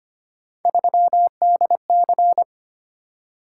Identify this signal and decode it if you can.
Morse code, heard as a keyed tone, about 25 wpm, 700 Hz. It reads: 3DC
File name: Three Letter-Number Combinations - 25wpm